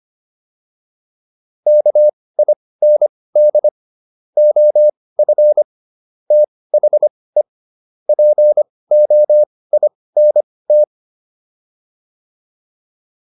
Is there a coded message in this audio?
Yes: Morse code from 1.7 to 10.8 s, heard as a keyed tone, about 25 wpm, spaced as for 18 wpm, 600 Hz. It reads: KIND OF THE POINT